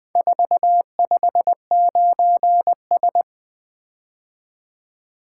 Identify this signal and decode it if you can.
Morse code, heard as a keyed tone, about 24 wpm, 700 Hz. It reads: 459S
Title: Speed-Racing - Four Letter-Number Combinations - 20wpm